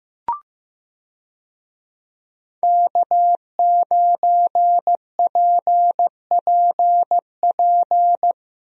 Morse code, heard as a keyed tone, about 15 wpm, 700 Hz. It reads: K9PPP
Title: USA Call Signs - 5 Characters - Repeated 3x - 15wpm